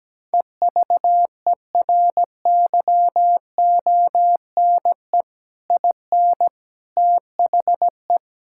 Morse code, heard as a keyed tone, about 17 wpm, 700 Hz. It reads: EVERYONE IN THE
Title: Sentences from Top 1000 Words - Review Part 6 - 17wpm